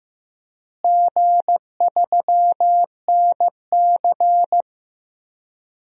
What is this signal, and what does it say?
Morse code, heard as a keyed tone, about 15 wpm, 700 Hz. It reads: G3NC
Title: DXCC-100 Call Signs - 4 Characters - Repeated 3x - 15wpm